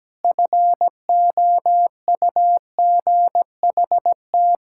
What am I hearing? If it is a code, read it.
Morse code, heard as a keyed tone, about 17 wpm, 700 Hz. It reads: FOUGHT
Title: Pangram Challenge v2 - 17wpm